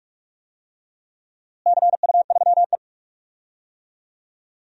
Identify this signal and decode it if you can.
Morse code, heard as a keyed tone, about 41 wpm, 700 Hz. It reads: CU3E
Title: DXCC-100 Call Signs - 4 Characters - Repeated 3x - 45wpm